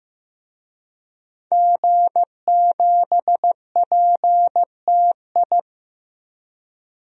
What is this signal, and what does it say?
Morse code, heard as a keyed tone, about 15 wpm, 700 Hz. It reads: G7PTI
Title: DXCC-100 Call Signs - 5 Characters - Repeated 3x - 15wpm